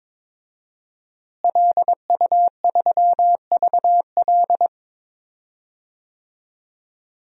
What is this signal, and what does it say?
Morse code, heard as a keyed tone, about 21 wpm, 700 Hz. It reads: LU3VL
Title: DXCC-100 Call Signs - 5 Characters - Repeated 2x - 22wpm